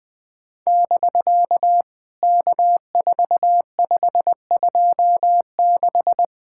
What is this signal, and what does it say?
Morse code, heard as a keyed tone, about 19 wpm, 700 Hz. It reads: <BK> K4526
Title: POTA QSO Practice - 20wpm